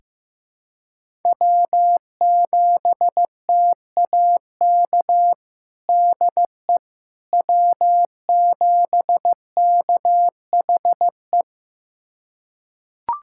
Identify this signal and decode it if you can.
Morse code, heard as a keyed tone, about 16 wpm, 700 Hz. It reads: W7TAK DE W7KHE
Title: USA Callsign DE Callsign - 5 Characters - 15wpm